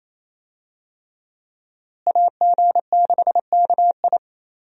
Morse code, heard as a keyed tone, about 27 wpm, 700 Hz. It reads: AG6KI
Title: USA Call Signs - 5 Characters - Repeated 3x - 28wpm